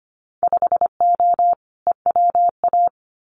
Morse code, heard as a keyed tone, about 25 wpm, 700 Hz. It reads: 5O EWA
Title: Winter Field Day Exchanges Encore 25wpm